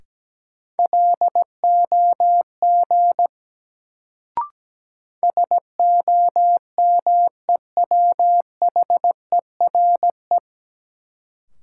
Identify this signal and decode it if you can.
Morse code, heard as a keyed tone, about 17 wpm, 700 Hz. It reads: LOG SOMEWHERE
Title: Sentences from Top 1400 Words - New Word Drill - 17wpm